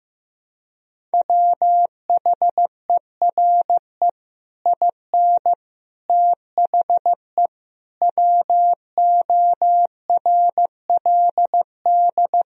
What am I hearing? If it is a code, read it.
Morse code, heard as a keyed tone, about 15 wpm, 700 Hz. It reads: WHERE IN THE WORLD
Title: Sentences from Top 200 Words and Abbreviations 15wpm